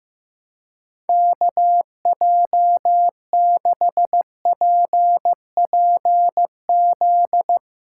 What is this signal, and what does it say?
Morse code, heard as a keyed tone, about 15 wpm, 700 Hz. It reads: KJ6PPZ
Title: USA Callsign DE Callsign 15wpm